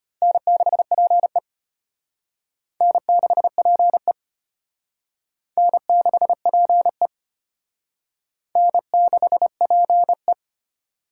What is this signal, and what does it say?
Morse code, heard as a keyed tone, about 32 wpm, 700 Hz. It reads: N6PE N6PE N6PE N6PE